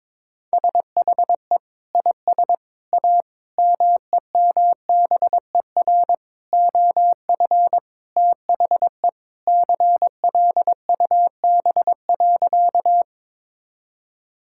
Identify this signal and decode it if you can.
Morse code, heard as a keyed tone, about 22 wpm, 700 Hz. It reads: SHE IS A MEMBER OF THE CLUB.
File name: practice / 22wpm / Sentences with 4-Word Phrases - 22wpm